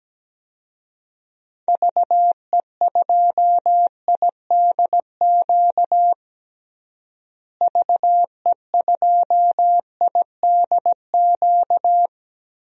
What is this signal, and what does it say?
Morse code, heard as a keyed tone, about 17 wpm, 700 Hz. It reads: VE2IDQ VE2IDQ
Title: DXCC-100 Call Signs - 6 Characters - Repeated 2x - 17wpm